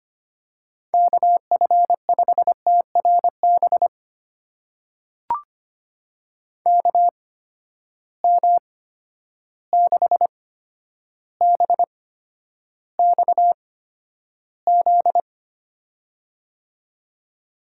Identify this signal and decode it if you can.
Morse code, heard as a keyed tone, about 25 wpm, spaced as for 11 wpm, 700 Hz. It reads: KF5TRB KM6BXZ
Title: USA Call Signs - 6 Characters - 8x Character Spacing - 25wpm